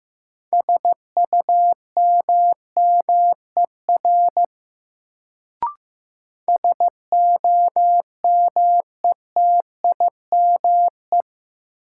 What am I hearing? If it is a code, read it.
Morse code, heard as a keyed tone, about 15 wpm, 700 Hz. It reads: SUMMER SOMETIME
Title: Sentences from Top 1000 Words - Review Part 6 - New Word Drill - 15wpm